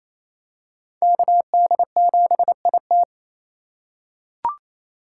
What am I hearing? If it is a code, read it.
Morse code, heard as a keyed tone, about 28 wpm, 700 Hz. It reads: KD7IT
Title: USA Call Signs - 2x Character Spacing - 28wpm